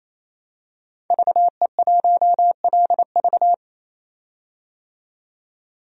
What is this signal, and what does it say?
Morse code, heard as a keyed tone, about 28 wpm, 700 Hz. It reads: VE1LV
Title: DXCC-100 Call Signs - 5 Characters - 28wpm